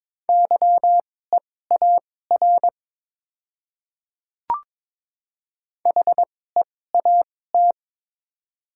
Morse code, heard as a keyed tone, about 22 wpm, spaced as for 16 wpm, 700 Hz. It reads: YEAR HEAT
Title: 2x Character Spacing - Top 500 Words - 22wpm